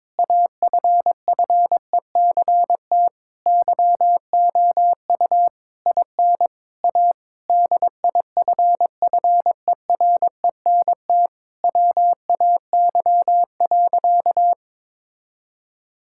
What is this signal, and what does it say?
Morse code, heard as a keyed tone, about 22 wpm, 700 Hz. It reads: AFFECT YOU IN A DIFFERENT WAY.
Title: Sentences from Top 1000 Words - Review Part 6 - 22wpm